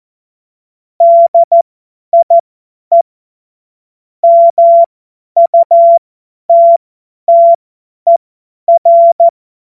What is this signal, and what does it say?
Morse code, heard as a keyed tone, about 14 wpm, spaced as for 10 wpm, 680 Hz. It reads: DIE MUTTER
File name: Morsekurs / Witze / Witz 27. September 2022 50 F70